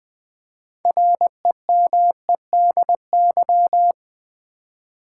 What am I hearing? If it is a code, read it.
Morse code, heard as a keyed tone, about 20 wpm, 700 Hz. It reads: REMEDY
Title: 6 Letter Words - First 3 Letters Given - 20wpm